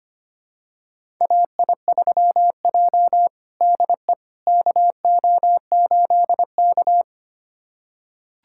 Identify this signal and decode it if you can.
Morse code, heard as a keyed tone, about 25 wpm, 700 Hz. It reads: AI3J DE KO8K
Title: USA Callsign DE Callsign - 4 Characters - 25wpm